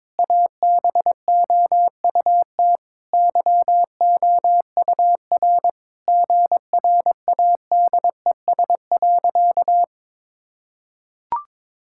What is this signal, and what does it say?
Morse code, heard as a keyed tone, about 22 wpm, 700 Hz. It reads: ABOUT YOUR GRADES.
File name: practice / 22wpm / Sentences with 4-Word Phrases - 22wpm